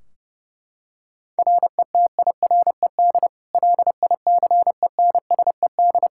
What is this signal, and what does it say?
Morse code, heard as a keyed tone, about 30 wpm, 700 Hz. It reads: RETIRED LICENSED
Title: Practice QSO - Exchange 3 - More Info - 30wpm